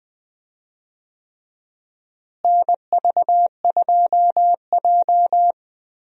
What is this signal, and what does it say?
Morse code, heard as a keyed tone, about 21 wpm, 700 Hz. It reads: NV2J
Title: USA Call Signs - 4 Characters - Repeated 3x - 20wpm